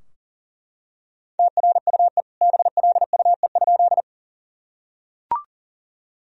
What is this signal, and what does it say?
Morse code, heard as a keyed tone, about 40 wpm, 700 Hz. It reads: TRUE BLUE?
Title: Sentences from Top 1000 Words - Review Part 3 - 40wpm